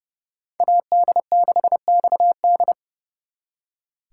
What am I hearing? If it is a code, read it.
Morse code, heard as a keyed tone, about 37 wpm, 700 Hz. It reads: AD6XD
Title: Speed Racing - USA 2x2 Call Signs 30wpm